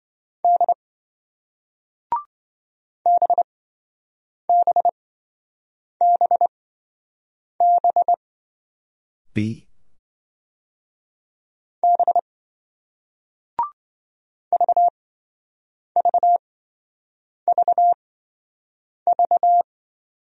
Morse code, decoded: D BBBB B VVVV

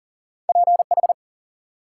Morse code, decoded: PH

